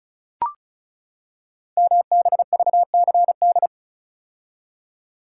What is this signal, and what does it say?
Morse code, heard as a keyed tone, about 36 wpm, 700 Hz. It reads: MBVCD
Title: Code Groups - 5 Letters - 35wpm